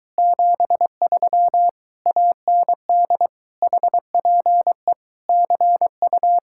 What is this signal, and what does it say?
Morse code, heard as a keyed tone, about 24 wpm, 700 Hz. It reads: 73 AND HPE CU